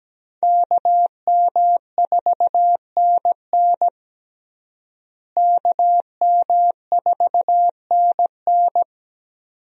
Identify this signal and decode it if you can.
Morse code, heard as a keyed tone, about 17 wpm, 700 Hz. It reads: KM4NN KM4NN
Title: USA Call Signs - 5 Characters - Repeated 2x - 17wpm